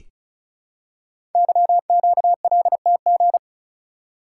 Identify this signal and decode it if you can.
Morse code, heard as a keyed tone, about 36 wpm, 700 Hz. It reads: YQLTG
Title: Code Groups - 5 Letters - 35wpm